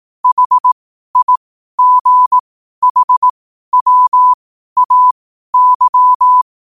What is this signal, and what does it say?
Morse code, heard as a keyed tone, about 18 wpm, spaced as for 12 wpm, 1000 Hz. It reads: HIGHWAY
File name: qotdfiles / 12wpm